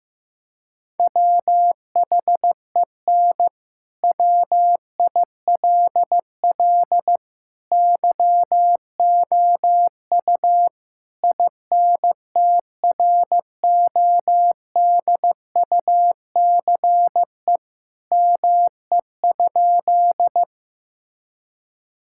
Morse code, heard as a keyed tone, about 15 wpm, 700 Hz. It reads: WHEN WILL YOU INTRODUCE ME?
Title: Sentences from Top 1600 Words 15wpm